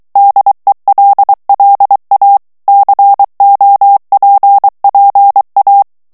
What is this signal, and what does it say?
Morse code, heard as a keyed tone, about 23 wpm, 800 Hz. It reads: DELLA COPPA
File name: ITALIEN texte clair 2 vitesse 22 mots minute